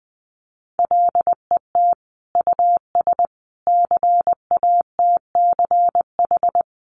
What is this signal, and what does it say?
Morse code, heard as a keyed tone, about 20 wpm, 700 Hz. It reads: LET US CATCH
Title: Sentences from Top 1500 Words 20wpm